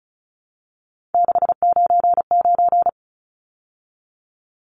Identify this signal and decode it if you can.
Morse code, heard as a keyed tone, about 34 wpm, 700 Hz. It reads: 699
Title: Numbers - 3-Digit 35wpm